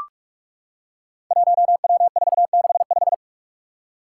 Morse code, heard as a keyed tone, about 44 wpm, 700 Hz. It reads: 1W465
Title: Five Challenging Letter-Number Combinations 45wpm